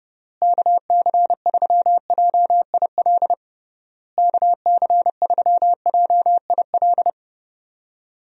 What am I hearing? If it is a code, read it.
Morse code, heard as a keyed tone, about 32 wpm, 700 Hz. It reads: KC3JIL KC3JIL
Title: USA Call Signs - 6 Characters - Repeated 3x - 30wpm